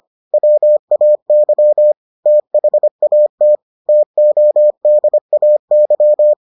Morse code, heard as a keyed tone, about 25 wpm, 600 Hz. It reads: WAY THAT TODAY